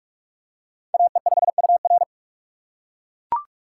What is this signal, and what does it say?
Morse code, heard as a keyed tone, about 44 wpm, 690 Hz. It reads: AE5UR
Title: USA Call Signs - 5 Characters - 6x Character Spacing - 45wpm